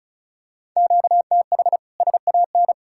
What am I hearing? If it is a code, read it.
Morse code, heard as a keyed tone, about 34 wpm, 700 Hz. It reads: QTH SAN
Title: Practice QSO - Exchange 1 - Beginning - 35wpm